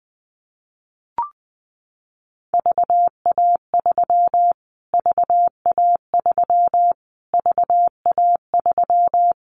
Morse code, heard as a keyed tone, about 20 wpm, 700 Hz. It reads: VA3 VA3 VA3
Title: Callsign Prefixes for Top 25 DXCC - Repeated 3x - 20wpm